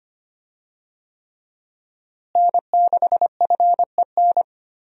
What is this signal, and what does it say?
Morse code, heard as a keyed tone, about 25 wpm, 700 Hz. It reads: N6FEN
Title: USA Call Signs - 5 Characters - Repeated 3x - 25wpm